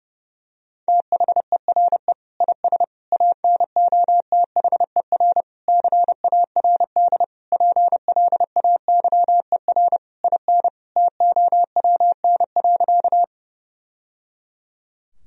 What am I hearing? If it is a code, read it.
Morse code, heard as a keyed tone, about 32 wpm, 700 Hz. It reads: THERE IS ANOTHER CARD PLAYER IN TOWN.
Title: Sentences from Top 1000 Words - Review Part 4 - 30wpm